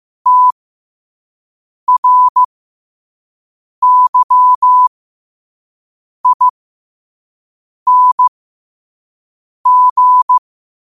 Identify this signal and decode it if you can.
Morse code, heard as a keyed tone, about 15 wpm, spaced as for 5 wpm, 1000 Hz. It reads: TRYING
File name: qotdfiles / 5wpm